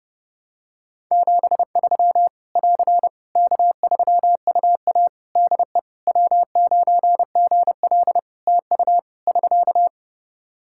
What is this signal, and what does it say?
Morse code, heard as a keyed tone, about 30 wpm, 700 Hz. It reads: 73 <AR> K3UA DE W9GL TU <SK>
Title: Practice QSO - Exchange 4 - Ending - 30wpm